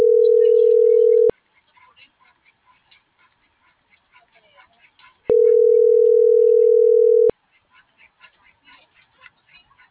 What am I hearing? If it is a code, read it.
Ringback tone (North America), 440 + 480 Hz.